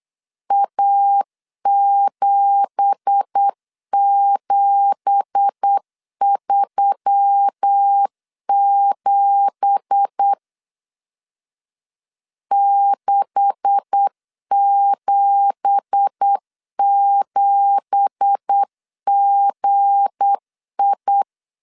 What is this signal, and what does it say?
Morse code, heard as a keyed tone, about 8 wpm, 800 Hz. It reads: A7737 677GI
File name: 8wpm